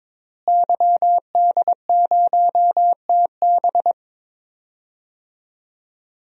Morse code, heard as a keyed tone, about 22 wpm, 700 Hz. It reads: YD0TB
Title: DXCC-100 Call Signs - 5 Characters - Repeated 2x - 22wpm